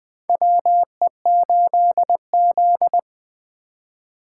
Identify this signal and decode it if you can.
Morse code, heard as a keyed tone, about 20 wpm, 700 Hz. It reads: WE8Z
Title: USA Call Signs - 4x Character Spacing - 20wpm